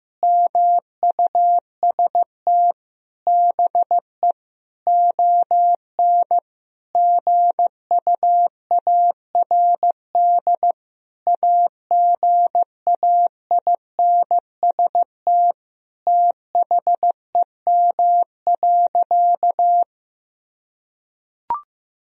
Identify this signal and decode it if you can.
Morse code, heard as a keyed tone, about 16 wpm, 700 Hz. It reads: MUST BE ON GUARD AGAINST THEM.
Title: Sentences from Top 1000 Words - Review Part 4 - 15wpm